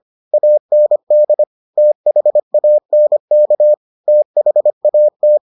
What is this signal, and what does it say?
Morse code, heard as a keyed tone, about 25 wpm, 600 Hz. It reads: AND THANK THAT